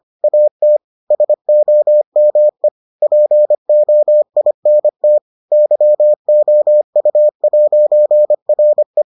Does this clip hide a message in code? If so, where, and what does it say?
Yes: Morse code from 0.2 to 9.0 s, heard as a keyed tone, about 25 wpm, 600 Hz. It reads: AT SOME POINT YOU'RE